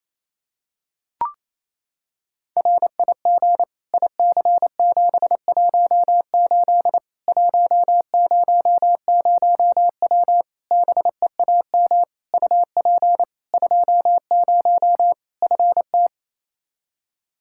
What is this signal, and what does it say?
Morse code, heard as a keyed tone, about 27 wpm, 700 Hz. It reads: RIG IC718 100W BEAM UP 20 FT
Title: K1USN SST QSO Practice 28wpm